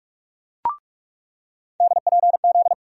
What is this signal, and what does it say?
Morse code, heard as a keyed tone, about 43 wpm, 700 Hz. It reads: DPZ
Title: USA Call Sign Postfixes - 3 Letters 45wpm